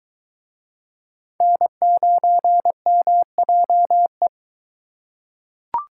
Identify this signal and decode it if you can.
Morse code, heard as a keyed tone, about 21 wpm, 700 Hz. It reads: N9MJE